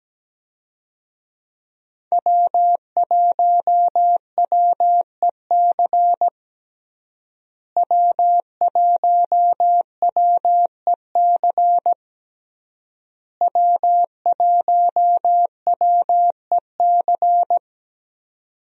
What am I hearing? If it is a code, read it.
Morse code, heard as a keyed tone, about 17 wpm, 700 Hz. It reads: W1WEC W1WEC W1WEC
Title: USA Call Signs - 5 Characters - Repeated 3x - 17wpm